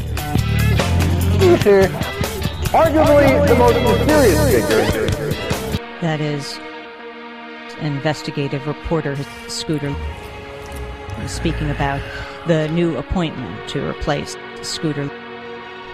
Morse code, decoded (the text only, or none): none